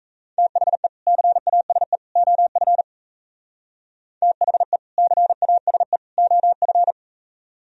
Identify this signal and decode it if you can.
Morse code, heard as a keyed tone, about 35 wpm, 700 Hz. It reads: THE CASE OF THE CASE OF
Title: Speed-Racing - Sets of 3 Words 28wpm